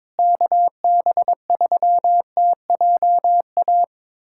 Morse code, heard as a keyed tone, about 21 wpm, 700 Hz. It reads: KB3TJA
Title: USA Callsign DE Callsign - 6 Characters - 22wpm